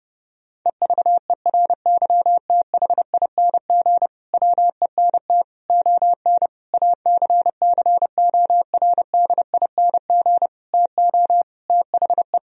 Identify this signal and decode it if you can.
Morse code, heard as a keyed tone, about 30 wpm, 700 Hz. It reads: EVERYTHING WENT ON ACCORDING TO THE
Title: Sentences from Top 1000 Words - Review Part 4 - 30wpm